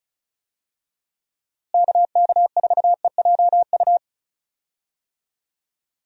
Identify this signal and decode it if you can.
Morse code, heard as a keyed tone, about 34 wpm, 700 Hz. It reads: KK4EJU
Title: USA Call Signs - 6 Characters - Repeated 3x - 35wpm